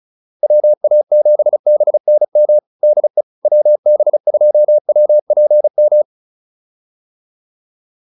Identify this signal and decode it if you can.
Morse code, heard as a keyed tone, about 34 wpm, 600 Hz. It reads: WA7BNM DE WB2WPM